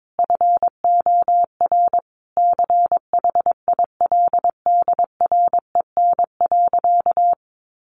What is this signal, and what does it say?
Morse code, heard as a keyed tone, about 22 wpm, 700 Hz. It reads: FOR CHILDREN.
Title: Sentences from Top 400 Words 22wpm